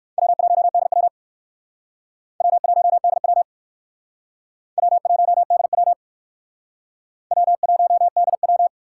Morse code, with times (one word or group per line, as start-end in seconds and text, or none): 0.2-1.1 W1DW
2.4-3.4 W1DW
4.8-5.9 W1DW
7.3-8.7 W1DW